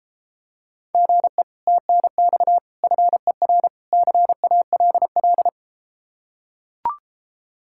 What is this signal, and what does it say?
Morse code, heard as a keyed tone, about 33 wpm, 700 Hz. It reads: GE TNX FER CALL